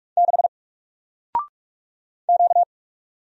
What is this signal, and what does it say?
Morse code, heard as a keyed tone, about 46 wpm, 700 Hz. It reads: 6 Q
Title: Single Letter-Number 45wpm